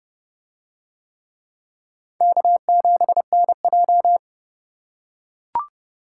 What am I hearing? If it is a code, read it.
Morse code, heard as a keyed tone, about 29 wpm, 700 Hz. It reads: K7NJ